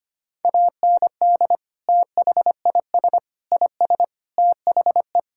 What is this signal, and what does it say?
Morse code, heard as a keyed tone, about 25 wpm, 700 Hz. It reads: AND THIS IS THE
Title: Sentences from Top 1000 Words - Review Part 5 - 25wpm